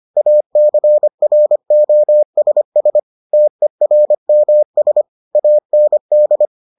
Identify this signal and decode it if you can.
Morse code, heard as a keyed tone, about 25 wpm, 600 Hz. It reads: ACROSS TERMS AND